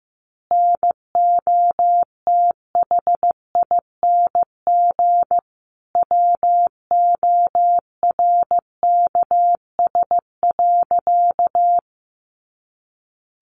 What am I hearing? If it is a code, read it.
Morse code, heard as a keyed tone, about 15 wpm, 700 Hz. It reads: NOTHING WORKS.
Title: Additional Easy Phrases 15wpm